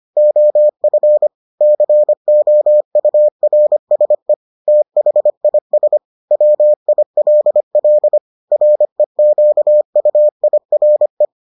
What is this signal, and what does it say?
Morse code, heard as a keyed tone, about 25 wpm, 600 Hz. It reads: OF COURSE THIS WILL REQUIRE